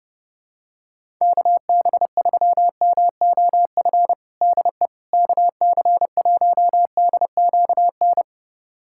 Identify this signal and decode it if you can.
Morse code, heard as a keyed tone, about 30 wpm, 700 Hz. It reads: KB3MOF DE KC1DQN
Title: USA Callsign DE Callsign 30wpm